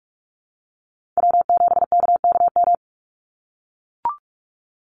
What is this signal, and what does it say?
Morse code, heard as a keyed tone, about 44 wpm, 700 Hz. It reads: W7KKM
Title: NA SOTA Chaser Callsigns 45wpm